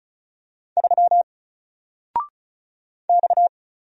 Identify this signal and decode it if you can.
Morse code, heard as a keyed tone, about 36 wpm, 700 Hz. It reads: 3 X
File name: Single-Letter-Number-35wpm